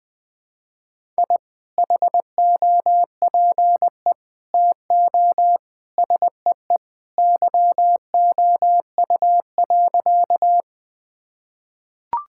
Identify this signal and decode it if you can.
Morse code, heard as a keyed tone, about 20 wpm, 700 Hz. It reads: I HOPE TO SEE YOU.
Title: Sentences with 2-Word Phrases - Spoken Beginning - 20wpm